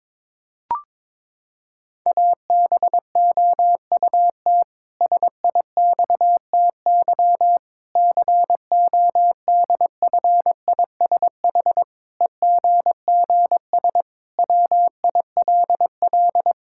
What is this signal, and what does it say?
Morse code, heard as a keyed tone, about 22 wpm, 700 Hz. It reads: ABOUT SIXTY CODFISH EGGS WILL
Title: Pangram Challenge v2 - 22wpm